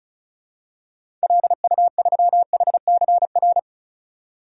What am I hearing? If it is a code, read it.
Morse code, heard as a keyed tone, about 36 wpm, 700 Hz. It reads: LU3HCR
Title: DXCC-100 Call Signs - 6 Characters - 35wpm